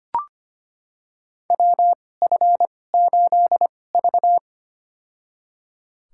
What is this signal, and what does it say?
Morse code, heard as a keyed tone, about 25 wpm, spaced as for 18 wpm, 700 Hz. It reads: WF8V
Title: USA Call Signs - 4 Characters - 2x Character Spacing - 25wpm